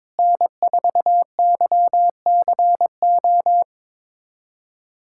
Morse code, heard as a keyed tone, about 22 wpm, 700 Hz. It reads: N4YCO
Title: USA Callsign DE Callsign - 5 Characters - 22wpm